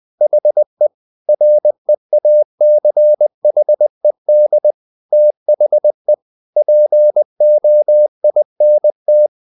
Morse code, heard as a keyed tone, about 20 wpm, 600 Hz. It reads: HE REACHED THE POINT
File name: the old man and the sea part 2 20wpm